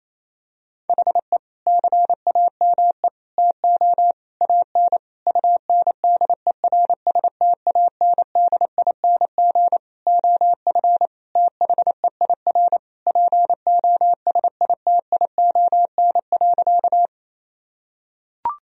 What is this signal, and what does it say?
Morse code, heard as a keyed tone, about 28 wpm, 700 Hz. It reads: HE CAME TO AN UNDERSTANDING OF THEIR POSITION.
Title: Sentences from Top 1000 Words - Review Part 3 - 28wpm